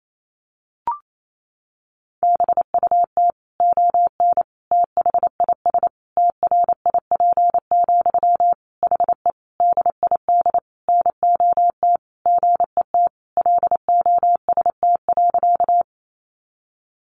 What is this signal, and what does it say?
Morse code, heard as a keyed tone, about 28 wpm, 700 Hz. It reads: BUT ON THIS TRIP, HE DID NOT GET LOST.
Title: Sentences from Top 1400 Words 28wpm